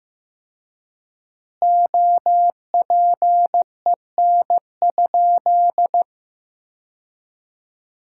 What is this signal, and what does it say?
Morse code, heard as a keyed tone, about 16 wpm, 700 Hz. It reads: OPEN?
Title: 4x Word-Spacing - Sentences from Top 300 Words - 15wpm